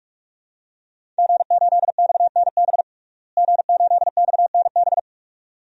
Morse code, heard as a keyed tone, about 46 wpm, 700 Hz. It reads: G8XNB G8XNB
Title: DXCC-100 Call Signs - 5 Characters - Repeated 2x - 45wpm